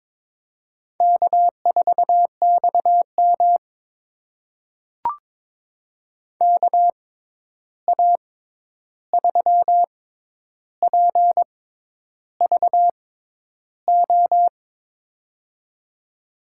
Morse code, heard as a keyed tone, about 21 wpm, spaced as for 10 wpm, 700 Hz. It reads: K4XM KA3PVO
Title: USA Call Signs - 6x Character Spacing - 22wpm